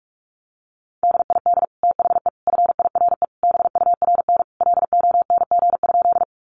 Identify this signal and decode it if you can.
Morse code, heard as a keyed tone, about 44 wpm, 700 Hz. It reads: DID THE FIRE BURN LONG?
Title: Sentences from Top 1300 Words 45wpm